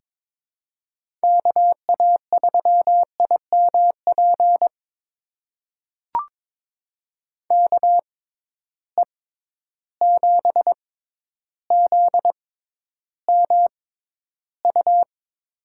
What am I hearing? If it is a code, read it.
Morse code, heard as a keyed tone, about 22 wpm, spaced as for 11 wpm, 700 Hz. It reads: KA3IMP KE7ZMU
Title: USA Call Signs - 6 Characters - 6x Character Spacing - 22wpm